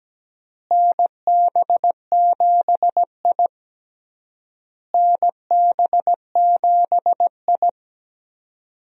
Morse code, heard as a keyed tone, about 17 wpm, 700 Hz. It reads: NB7I NB7I
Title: USA Call Signs - 4 Characters - Repeated 3x - 17wpm